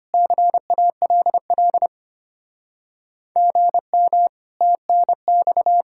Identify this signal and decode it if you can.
Morse code, heard as a keyed tone, about 32 wpm, 700 Hz. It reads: CALL GM TNX